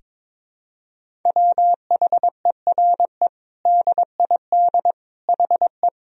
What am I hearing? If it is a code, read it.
Morse code, heard as a keyed tone, about 22 wpm, 700 Hz. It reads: WHERE DID HE
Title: Sentences from Top 200 Words and Abbreviations 22wpm